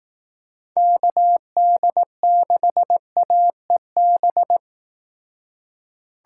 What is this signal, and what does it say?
Morse code, heard as a keyed tone, about 18 wpm, 700 Hz. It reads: KD6AEB
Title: Speed Racing - USA 2x3 Call Signs 15wpm